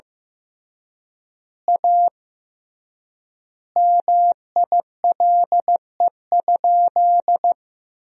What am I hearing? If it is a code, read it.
Morse code, heard as a keyed tone, about 15 wpm, 700 Hz. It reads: A MILE?
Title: 3x Word-Spacing - Sentences from Top 400 Words - 15wpm